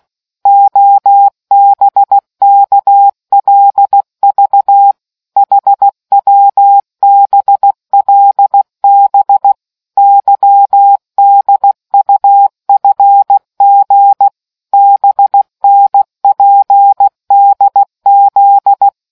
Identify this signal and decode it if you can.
Morse code, heard as a keyed tone, about 16 wpm, 800 Hz. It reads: OBKLV HWBLB YDUFG BNPDZ